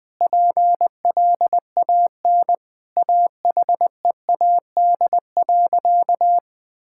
Morse code, heard as a keyed tone, about 20 wpm, 700 Hz. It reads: PLAN AHEAD.
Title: Sentences with 7-Word Phrases - 20wpm